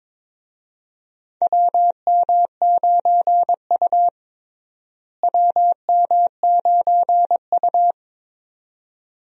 Morse code, heard as a keyed tone, about 22 wpm, 700 Hz. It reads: WM9U WM9U
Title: USA Call Signs - 4 Characters - Repeated 2x - 22wpm